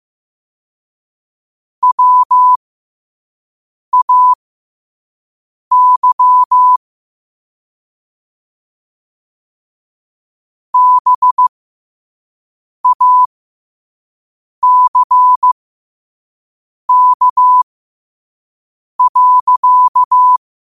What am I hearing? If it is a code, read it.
Morse code, heard as a keyed tone, about 15 wpm, spaced as for 5 wpm, 1000 Hz. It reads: WAY BACK.
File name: qotdfiles / 5wpm